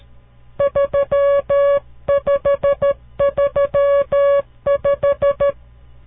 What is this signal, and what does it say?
Morse code, heard as a keyed tone, about 13 wpm, 560 Hz. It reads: 3535
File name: CW Learning Programs / farnsworth / unit 5